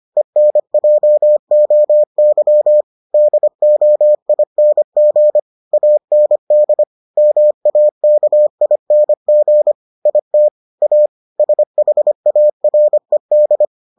Morse code, heard as a keyed tone, about 25 wpm, 600 Hz. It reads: ENJOY DOING AND MAKING IT A SHARED